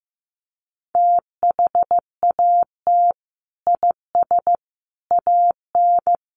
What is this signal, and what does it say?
Morse code, heard as a keyed tone, about 15 wpm, 700 Hz. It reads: THAT IS AN
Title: Sentences from Top 1300 Words 15wpm